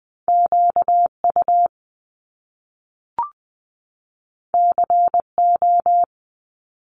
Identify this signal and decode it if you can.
Morse code, heard as a keyed tone, about 20 wpm, 700 Hz. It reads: QU CO
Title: Top Two Letter Combinations in Words 20wpm